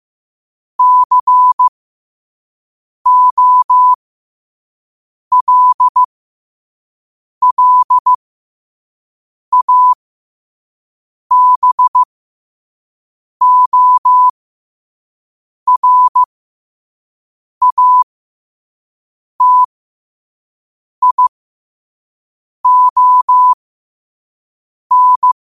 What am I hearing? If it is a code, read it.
Morse code, heard as a keyed tone, about 15 wpm, spaced as for 5 wpm, 1000 Hz. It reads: COLLABORATION
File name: qotdfiles / 5wpm